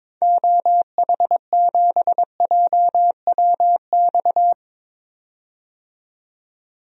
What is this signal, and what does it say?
Morse code, heard as a keyed tone, about 22 wpm, 700 Hz. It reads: OH7JWX
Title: DXCC-100 Call Signs - 6 Characters - 22wpm